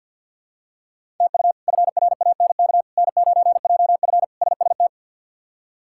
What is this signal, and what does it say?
Morse code, heard as a keyed tone, about 51 wpm, 700 Hz. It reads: TU FRANK N9JF SST
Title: K1USN SST QSO Practice - 50wpm